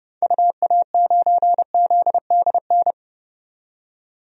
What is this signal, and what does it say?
Morse code, heard as a keyed tone, about 30 wpm, 700 Hz. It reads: UA9ZDN
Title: DXCC-100 Call Signs - 6 Characters - 30wpm